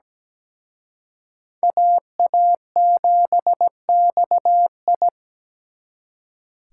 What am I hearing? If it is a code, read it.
Morse code, heard as a keyed tone, about 20 wpm, 700 Hz. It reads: AA7XI
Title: Speed Racing - USA 2x2 Call Signs 17wpm